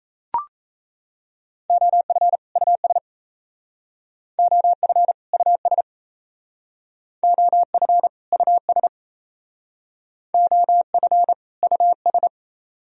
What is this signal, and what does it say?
Morse code, heard as a keyed tone, about 35 wpm, 700 Hz. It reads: OF US OF US OF US OF US